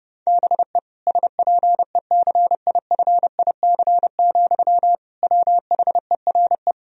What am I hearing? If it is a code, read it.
Morse code, heard as a keyed tone, about 30 wpm, 700 Hz. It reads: BE SPECIFIC, WHERE
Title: Sentences from Top 1000 Words - Review Part 3 - 30wpm